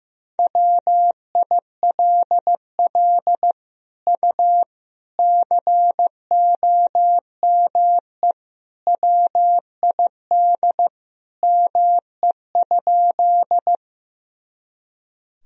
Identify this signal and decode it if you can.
Morse code, heard as a keyed tone, about 15 wpm, 700 Hz. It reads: WILL U COME WID ME?
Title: Sentences from Top 200 Words and Abbreviations 15wpm